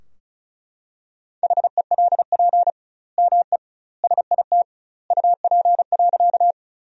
Morse code, heard as a keyed tone, about 36 wpm, 700 Hz. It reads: HELP ME SIT UP.
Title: 2x Word-Spacing - Sentences from Top 500 Words - 35wpm